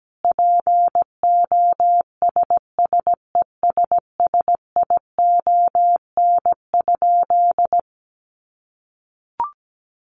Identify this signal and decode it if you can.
Morse code, heard as a keyed tone, about 17 wpm, 700 Hz. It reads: POSSESSION?
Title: Sentences from Top 1400 Words 17wpm